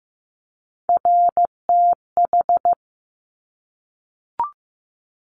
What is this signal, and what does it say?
Morse code, heard as a keyed tone, about 15 wpm, 700 Hz. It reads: RTH
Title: Top Three Letter Combinations in Words 15wpm